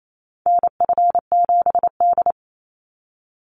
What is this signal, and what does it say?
Morse code, heard as a keyed tone, about 28 wpm, 700 Hz. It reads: NF7D
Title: Call Signs - 28wpm